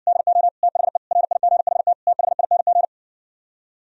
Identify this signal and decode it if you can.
Morse code, heard as a keyed tone, about 67 wpm, 700 Hz. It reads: DO THE RIGHT THING